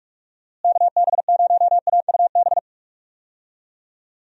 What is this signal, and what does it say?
Morse code, heard as a keyed tone, about 44 wpm, 700 Hz. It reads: KB0AUB
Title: USA Call Signs - 6 Characters - Repeated 3x - 45wpm